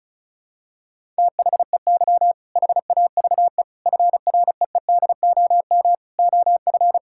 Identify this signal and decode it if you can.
Morse code, heard as a keyed tone, about 34 wpm, 700 Hz. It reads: THEY HAVE FREEDOM OF
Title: Sentences from Top 1000 Words - Review Part 5 - 35wpm